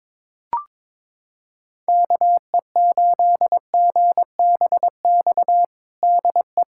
Morse code, heard as a keyed tone, about 22 wpm, 700 Hz. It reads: KE8GBX DE
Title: USA Callsign DE Callsign - 6 Characters - 22wpm